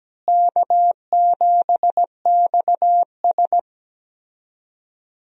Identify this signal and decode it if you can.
Morse code, heard as a keyed tone, about 17 wpm, 700 Hz. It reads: K7XS
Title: USA Call Signs - 4 Characters - Repeated 2x - 17wpm